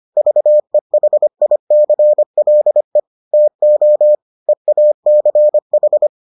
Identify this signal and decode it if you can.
Morse code, heard as a keyed tone, about 25 wpm, 600 Hz. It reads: VEHICLE TO EACH